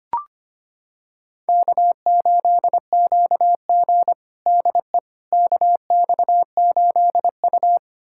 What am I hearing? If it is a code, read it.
Morse code, heard as a keyed tone, about 25 wpm, 700 Hz. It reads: K8QG DE KX8U
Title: USA Callsign DE Callsign - 4 Characters - 25wpm